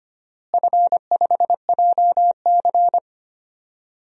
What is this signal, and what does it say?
Morse code, heard as a keyed tone, about 25 wpm, 700 Hz. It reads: F5JC
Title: DXCC-100 Call Signs - 4 Characters - Repeated 3x - 25wpm